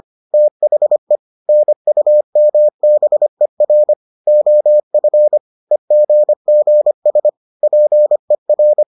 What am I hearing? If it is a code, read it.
Morse code, heard as a keyed tone, about 25 wpm, 600 Hz. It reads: THE NUMBER OF EGGS PER